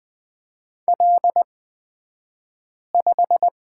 Morse code, heard as a keyed tone, about 20 wpm, spaced as for 5 wpm, 700 Hz. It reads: L5
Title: 223 Lesson 36d - BT - Two Characters 20wpm